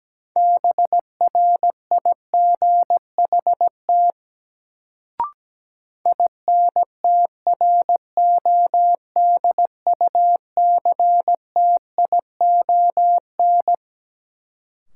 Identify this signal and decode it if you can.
Morse code, heard as a keyed tone, about 17 wpm, 700 Hz. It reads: BRIGHT INTRODUCTION
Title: Sentences from Top 1000 Words - New Word Drill - 17wpm